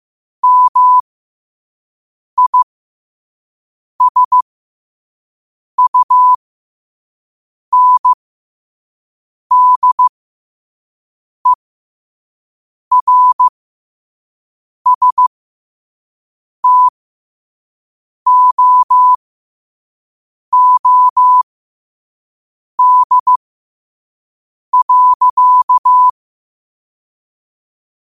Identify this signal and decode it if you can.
Morse code, heard as a keyed tone, about 15 wpm, spaced as for 5 wpm, 1000 Hz. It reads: MISUNDERSTOOD.
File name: qotdfiles / 5wpm